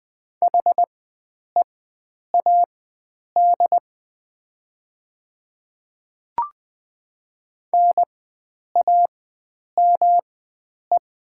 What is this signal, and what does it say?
Morse code, heard as a keyed tone, about 20 wpm, spaced as for 9 wpm, 700 Hz. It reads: HEAD NAME
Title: 4x Character Spacing - Top 300 Words - 20wpm